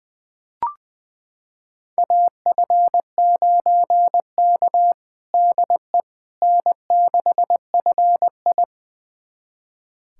Morse code, heard as a keyed tone, about 20 wpm, 700 Hz. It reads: AF9K DE N6FI
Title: USA Callsign DE Callsign - 4 Characters - 20wpm